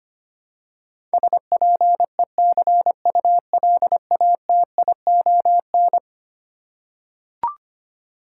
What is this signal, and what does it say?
Morse code, heard as a keyed tone, about 25 wpm, 700 Hz. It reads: SPECULATION
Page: morse-code-ninja-practice-25wpm